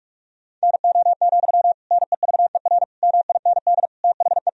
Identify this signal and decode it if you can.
Morse code, heard as a keyed tone, about 46 wpm, 700 Hz. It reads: NO, NEVER MIND THE
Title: Sentences from Top 1000 Words - Review Part 1 - 45wpm